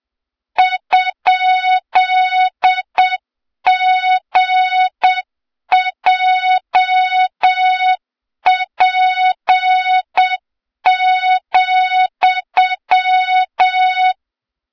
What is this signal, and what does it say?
Morse code, heard as a keyed tone, about 7 wpm, 3700 Hz, with noise. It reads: ?GJP,
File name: Morze-HA2MN / hang / 23 GYAKORLAT